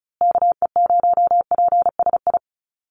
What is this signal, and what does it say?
Morse code, heard as a keyed tone, about 34 wpm, 700 Hz. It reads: KE0PSI
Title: USA Call Signs Repeated 2x 35wpm